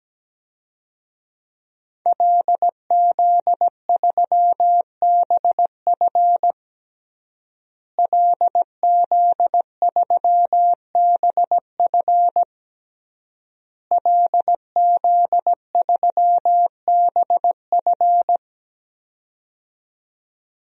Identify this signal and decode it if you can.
Morse code, heard as a keyed tone, about 17 wpm, 700 Hz. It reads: LZ3BF LZ3BF LZ3BF
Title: DXCC-100 Call Signs - 5 Characters - Repeated 3x - 17wpm